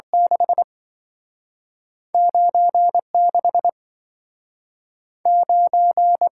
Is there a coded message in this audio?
Yes: Morse code from 0.1 to 6.3 s, heard as a keyed tone, about 27 wpm, 700 Hz. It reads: 6 96 9